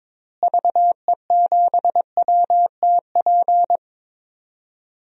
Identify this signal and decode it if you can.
Morse code, heard as a keyed tone, about 22 wpm, 700 Hz. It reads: VE7WTP
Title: DXCC-100 Call Signs - 6 Characters - Repeated 3x - 22wpm